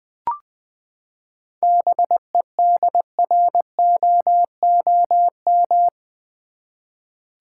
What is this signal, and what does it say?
Morse code, heard as a keyed tone, about 20 wpm, 700 Hz. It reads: BEDROOM
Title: Seven Letter Words - 20wpm